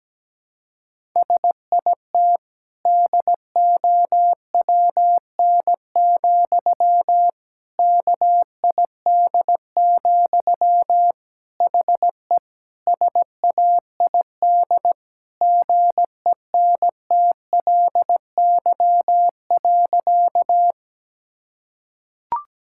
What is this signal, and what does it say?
Morse code, heard as a keyed tone, about 17 wpm, 700 Hz. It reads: SIT DOWN, KID, HE SAID GENTLY.
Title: Sentences from Top 1600 Words 17wpm